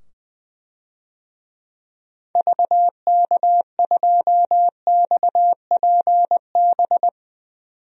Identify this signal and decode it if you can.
Morse code, heard as a keyed tone, about 20 wpm, 700 Hz. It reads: VK2XPB
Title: DXCC-100 Call Signs - 6 Characters - 20wpm